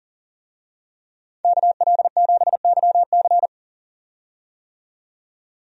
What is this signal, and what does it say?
Morse code, heard as a keyed tone, about 40 wpm, 700 Hz. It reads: KL7YC
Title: DXCC-100 Call Signs - 5 Characters - 40wpm